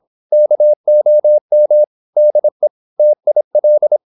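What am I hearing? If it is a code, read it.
Morse code, heard as a keyed tone, about 26 wpm, 600 Hz. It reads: KOM DE TIL
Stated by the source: High-speed (QRQ) Morse at 26 wpm.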